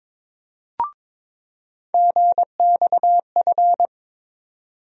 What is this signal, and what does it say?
Morse code, heard as a keyed tone, about 22 wpm, 700 Hz. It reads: GXF
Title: USA Call Sign Postfixes - 3 Letters 22wpm